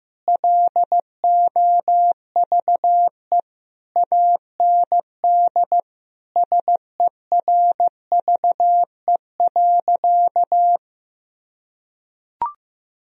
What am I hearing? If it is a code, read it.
Morse code, heard as a keyed tone, about 15 wpm, 700 Hz. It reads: LOVE AND SERVE.
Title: Sentences from Top 1000 Words - Review Part 2 - 15wpm